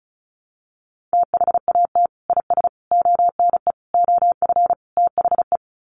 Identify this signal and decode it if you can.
Morse code, heard as a keyed tone, about 34 wpm, 700 Hz. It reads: THAT IS ONE OF THE
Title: Sets of 5 Words Encore 35wpm